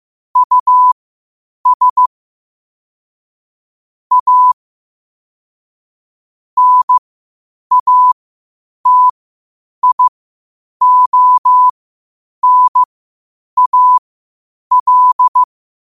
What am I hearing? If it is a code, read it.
Morse code, heard as a keyed tone, about 15 wpm, spaced as for 8 wpm, 1000 Hz. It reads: US A NATIONAL
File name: 8wpm